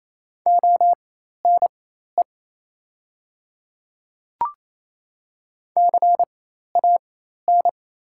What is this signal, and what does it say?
Morse code, heard as a keyed tone, about 28 wpm, spaced as for 13 wpm, 700 Hz. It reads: ONE CAN
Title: 4x Character Spacing - Top 100 Words - 28wpm